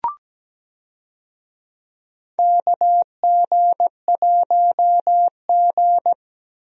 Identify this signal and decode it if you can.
Morse code, heard as a keyed tone, about 17 wpm, 700 Hz. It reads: KG1G
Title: USA Call Signs - 4 Characters - Repeated 2x - 17wpm